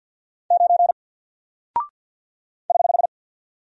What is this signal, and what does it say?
Morse code, heard as a keyed tone, about 49 wpm, 700 Hz. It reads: 9 <HH>
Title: Single Character (Letter:Number:Punctuation:Prosign) Alt 50wpm